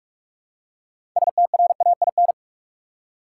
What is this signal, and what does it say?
Morse code, heard as a keyed tone, about 46 wpm, 700 Hz. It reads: STRAIN
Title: Six Letter Words - 45wpm